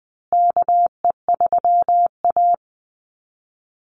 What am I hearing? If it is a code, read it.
Morse code, heard as a keyed tone, about 20 wpm, 700 Hz. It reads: KE3A
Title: USA Call Signs Repeated 2x 20wpm